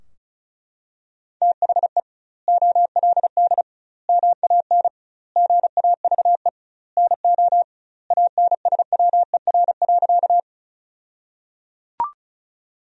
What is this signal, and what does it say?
Morse code, heard as a keyed tone, about 36 wpm, 700 Hz. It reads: THE OLD MAN GAVE NO ANSWER.
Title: 2x Word-Spacing - Sentences from Top 500 Words - 35wpm